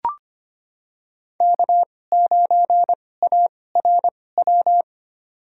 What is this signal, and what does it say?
Morse code, heard as a keyed tone, about 25 wpm, spaced as for 18 wpm, 700 Hz. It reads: K9ARW
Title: USA Call Signs - 5 Characters - 2x Character Spacing - 25wpm